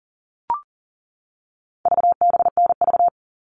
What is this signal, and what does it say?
Morse code, heard as a keyed tone, about 37 wpm, 700 Hz. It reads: VBNV
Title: All Cut Numbers - 4-Digit 40wpm